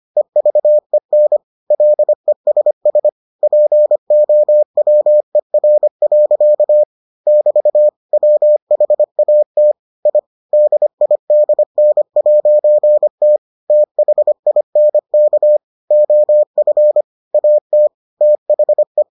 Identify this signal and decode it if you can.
Morse code, heard as a keyed tone, about 25 wpm, 600 Hz. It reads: EVEN LESS POWER. <BT> WHAT I DIDN'T THINK OF AT THE